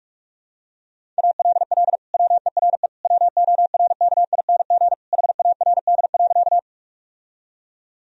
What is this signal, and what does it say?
Morse code, heard as a keyed tone, about 46 wpm, 700 Hz. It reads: ALL WERE WORKING HARD.
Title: Sentences from Top 1000 Words - Review Part 4 - 45wpm